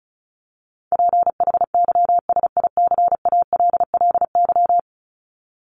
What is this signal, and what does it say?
Morse code, heard as a keyed tone, about 34 wpm, 700 Hz. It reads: PHYSICALLY
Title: Top 1501-1750 Words 35wpm